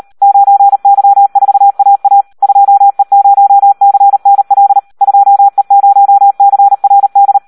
Morse code, heard as a keyed tone, about 38 wpm, 800 Hz. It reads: 9Y4AA 2E0CNL 2E0CRD
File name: Audio practica CW / SD 1 38